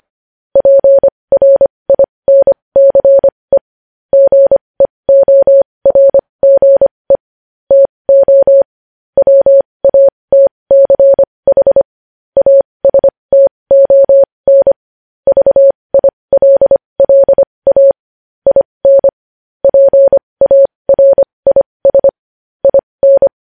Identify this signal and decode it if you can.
Morse code, heard as a keyed tone, about 25 wpm, spaced as for 20 wpm, 550 Hz. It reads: PRINCE GEORGE TO WATCH ASTON VILLA IN PARIS IN
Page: news headlines in morse code at 20 wpm